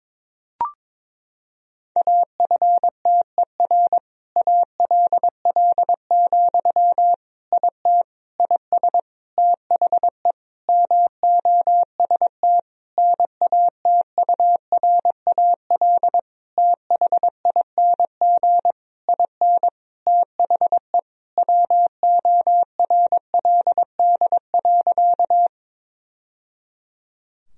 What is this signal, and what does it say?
Morse code, heard as a keyed tone, about 21 wpm, 700 Hz. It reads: AFTER ALL, IT IS THE MOST NATURAL THING IN THE WORLD.
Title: Sentences from Top 1000 Words - Review Part 4 - 22wpm